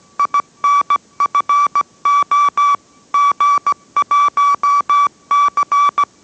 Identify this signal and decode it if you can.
Morse code, heard as a keyed tone, about 17 wpm, 1150 Hz. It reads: INFO G1C